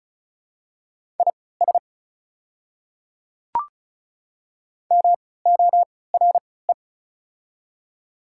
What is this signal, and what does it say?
Morse code, heard as a keyed tone, about 36 wpm, spaced as for 20 wpm, 690 Hz. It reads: IS MORE
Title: 3x Character Spacing - Top 200 Words - 35wpm